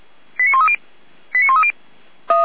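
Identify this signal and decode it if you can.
Five-tone selective call (ZVEI), heard as quick runs of five tones, tones of about 70 ms. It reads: 89130 89130
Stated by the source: ZVEI five-tone selective call 89305 received by SDR on channel 455, from feuerwehr-orlamuende.org fire brigade audio archive.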